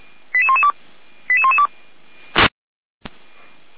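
Five-tone selective call (ZVEI), heard as quick runs of five tones, tones of about 70 ms. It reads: 88112 88112
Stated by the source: ZVEI five-tone selective call 88191, received on SDR.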